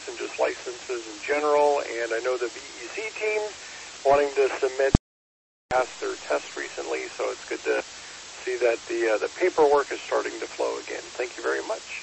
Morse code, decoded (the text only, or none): none